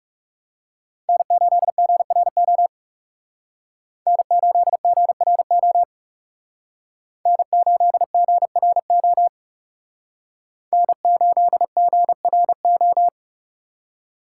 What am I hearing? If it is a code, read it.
Morse code, heard as a keyed tone, about 38 wpm, 700 Hz. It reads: N8GRO N8GRO N8GRO N8GRO